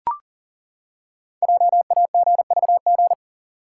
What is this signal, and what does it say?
Morse code, heard as a keyed tone, about 40 wpm, 700 Hz. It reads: JAGVG
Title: Code Groups - 5 Letters - 40wpm